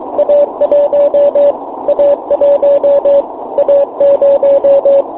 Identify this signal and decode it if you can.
Morse code, heard as a keyed tone, about 23 wpm, 600 Hz, with radio noise. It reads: A1 A1 A0
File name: fo-29 cw bake